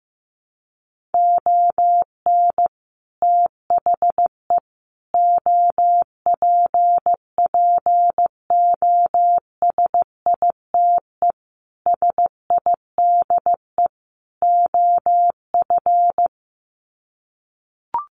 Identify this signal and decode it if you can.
Morse code, heard as a keyed tone, about 14 wpm, 700 Hz. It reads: ON THE OPPOSITE SIDE OF
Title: Sets of 5 Words Encore2 15wpm